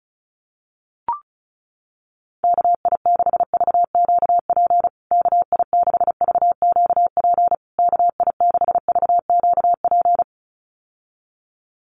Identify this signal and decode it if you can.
Morse code, heard as a keyed tone, about 34 wpm, 700 Hz. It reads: KI6VQP KI6VQP KI6VQP
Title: USA Call Signs Repeated 3x 35wpm